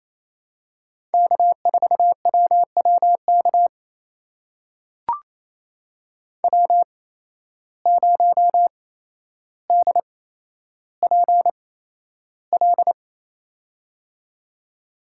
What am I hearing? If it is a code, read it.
Morse code, heard as a keyed tone, about 28 wpm, spaced as for 12 wpm, 700 Hz. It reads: K4WWK W0DPL
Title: USA Call Signs - 5 Characters - 8x Character Spacing - 28wpm